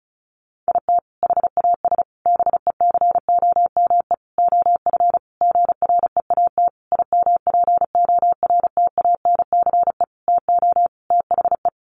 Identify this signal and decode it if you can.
Morse code, heard as a keyed tone, about 34 wpm, 700 Hz. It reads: IT HAS BECOME OF GREAT IMPORTANCE TO THE
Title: Sentences from Top 1200 Words 35wpm